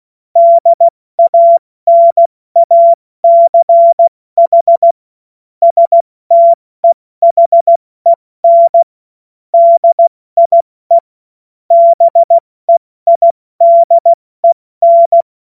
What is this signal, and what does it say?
Morse code, heard as a keyed tone, about 16 wpm, 680 Hz. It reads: DANACH STEHEN DIE BEIDEN
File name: Morsekurs / Witze / Witz 22. August 2023 70 F80